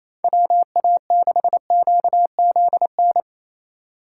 Morse code, heard as a keyed tone, about 34 wpm, 700 Hz. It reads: WA6QZN
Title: Speed Racing - USA 2x3 Call Signs 28wpm